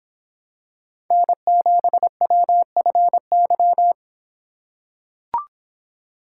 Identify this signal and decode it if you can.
Morse code, heard as a keyed tone, about 26 wpm, 700 Hz. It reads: N7WFY